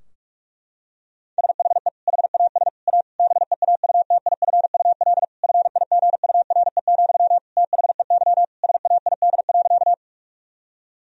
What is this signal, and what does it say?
Morse code, heard as a keyed tone, about 42 wpm, 700 Hz. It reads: SHE HAS A BEAUTIFUL FIGURE, THEY SAID.
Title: Sentences from Top 1000 Words - Review Part 2 - 45wpm